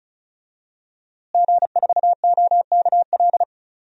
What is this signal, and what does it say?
Morse code, heard as a keyed tone, about 36 wpm, 700 Hz. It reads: G4OKL
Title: DXCC-100 Call Signs - 5 Characters - Repeated 3x - 35wpm